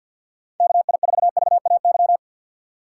Morse code, heard as a keyed tone, about 46 wpm, 700 Hz. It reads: KI4VAY
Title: USA Call Signs - 6 Characters - 4x Character Spacing - 50wpm